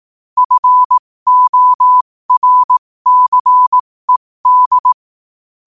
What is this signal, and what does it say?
Morse code, heard as a keyed tone, about 18 wpm, spaced as for 15 wpm, 1000 Hz. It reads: FORCED
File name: 15wpm